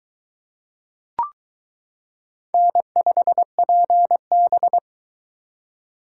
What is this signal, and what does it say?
Morse code, heard as a keyed tone, about 22 wpm, 700 Hz. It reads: N5PB